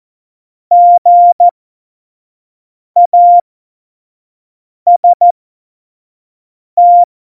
Morse code, heard as a keyed tone, about 14 wpm, spaced as for 5 wpm, 700 Hz. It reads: GAST
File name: Morsekurs / Witze / Witz 14. Januar 2020, 25 F70